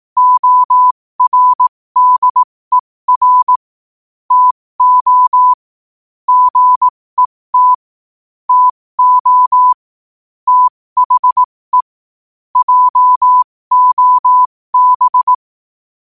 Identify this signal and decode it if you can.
Morse code, heard as a keyed tone, about 18 wpm, spaced as for 15 wpm, 1000 Hz. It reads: ORDER TO GET TO THE JOB
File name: qotdfiles / 15wpm